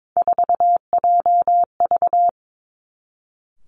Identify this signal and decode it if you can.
Morse code, heard as a keyed tone, about 22 wpm, 700 Hz. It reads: 4JV
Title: Three Challenging Letter-Number Combinations 22wpm